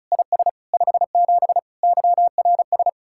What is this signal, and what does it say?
Morse code, heard as a keyed tone, about 46 wpm, 700 Hz. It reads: IS 57 YRS